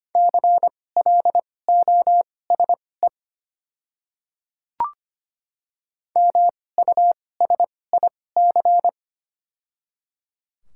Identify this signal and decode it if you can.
Morse code, heard as a keyed tone, about 25 wpm, spaced as for 18 wpm, 700 Hz. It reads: CLOSE MUSIC
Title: 2x Character Spacing - Top 300 Words - 25wpm